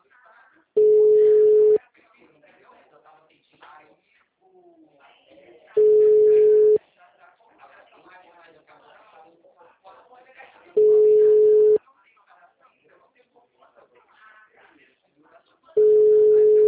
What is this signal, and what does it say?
Ringback tone (Europe, 425 Hz), with speech.